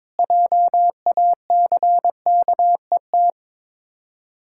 Morse code, heard as a keyed tone, about 22 wpm, 700 Hz. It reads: JACKET
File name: Difficult Words with Hints - 22wpm